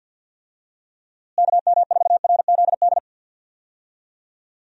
Morse code, heard as a keyed tone, about 50 wpm, 700 Hz. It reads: KM4RZD
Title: USA Call Signs - 6 Characters - Repeated 2x - 50wpm